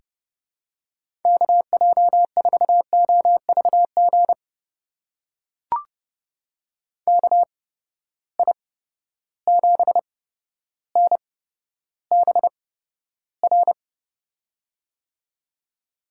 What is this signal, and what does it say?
Morse code, heard as a keyed tone, about 30 wpm, spaced as for 13 wpm, 700 Hz. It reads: KJ4OVG KI7NBR
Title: USA Call Signs - 6 Characters - 8x Character Spacing - 30wpm